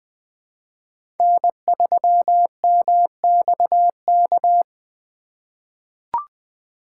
Morse code, heard as a keyed tone, about 20 wpm, 700 Hz. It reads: N3MXK
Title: USA Call Signs - 5 Characters - 4x Character Spacing - 20wpm